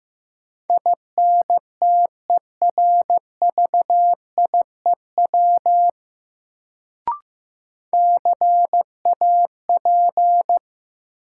Morse code, heard as a keyed tone, about 15 wpm, 700 Hz. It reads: INTERVIEW CAP
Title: Sentences from Top 1600 Words - New Word Drill - 15wpm